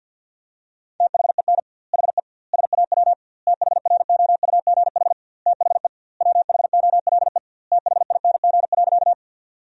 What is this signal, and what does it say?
Morse code, heard as a keyed tone, about 48 wpm, 700 Hz. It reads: THEN HE SAW THROUGH THE WHOLE THING.
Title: 2x Word-Spacing - Sentences from Top 400 Words - 50wpm